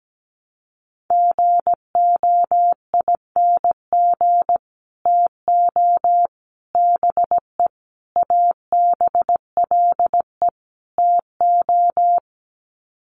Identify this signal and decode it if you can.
Morse code, heard as a keyed tone, about 17 wpm, 700 Hz. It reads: GOING TO BE ABLE TO
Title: Sets of 5 Words 17wpm